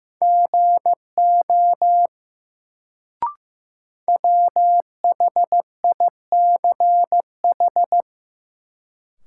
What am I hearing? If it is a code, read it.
Morse code, heard as a keyed tone, about 15 wpm, 700 Hz. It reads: GO WHICH
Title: Sentences from Top 100 Words - New Word Drill - 15wpm